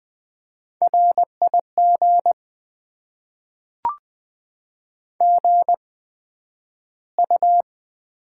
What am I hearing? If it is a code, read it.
Morse code, heard as a keyed tone, about 20 wpm, spaced as for 11 wpm, 700 Hz. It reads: RIG GU
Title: Top 20 QSO Elements - 8x Character Spacing - 20wpm